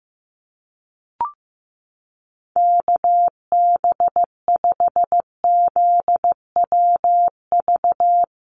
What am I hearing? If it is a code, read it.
Morse code, heard as a keyed tone, about 15 wpm, 700 Hz. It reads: KB5ZWV
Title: USA Call Signs Repeated 3x 15wpm